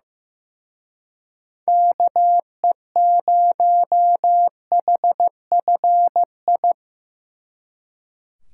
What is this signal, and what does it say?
Morse code, heard as a keyed tone, about 18 wpm, 700 Hz. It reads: KE0HFI
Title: Speed Racing - USA Call Signs 15wpm